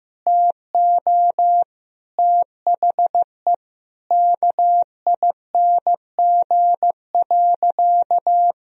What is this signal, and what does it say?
Morse code, heard as a keyed tone, about 15 wpm, 700 Hz. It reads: TO THE KING.
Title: Sentences from Top 1000 Words - Review Part 4 - 15wpm